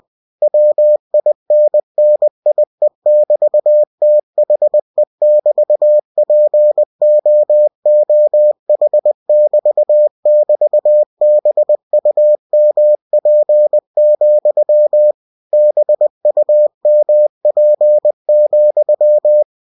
Morse code, heard as a keyed tone, about 20 wpm, 600 Hz. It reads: WINNIE<BT>THE<BT>POOH<BT><BT>BUMP, BUMP,
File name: cw-books / winnie-the-pooh / winnie-20wpm-page-0111